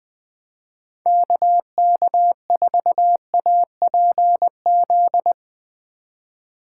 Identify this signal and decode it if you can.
Morse code, heard as a keyed tone, about 21 wpm, 700 Hz. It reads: KK4APZ
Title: Speed Racing - USA 2x3 Call Signs 20wpm